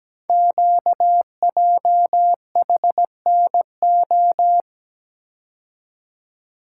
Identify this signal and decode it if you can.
Morse code, heard as a keyed tone, about 17 wpm, 700 Hz. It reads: QJHNO
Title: Code Groups - 5 Letters - 17wpm